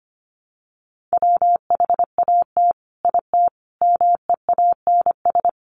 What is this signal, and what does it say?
Morse code, heard as a keyed tone, about 25 wpm, 700 Hz. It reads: WHAT IT MEANS
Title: Sets of 5 Words 25wpm